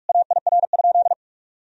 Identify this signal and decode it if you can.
Morse code, heard as a keyed tone, about 46 wpm, 700 Hz. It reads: AIR?
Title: Sentences from Top 1000 Words - Review Part 2 - 45wpm